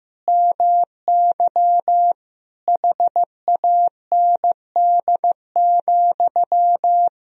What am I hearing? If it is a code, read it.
Morse code, heard as a keyed tone, about 15 wpm, 700 Hz. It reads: MY HAND,
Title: Sentences from Top 1000 Words - Review Part 3 - 15wpm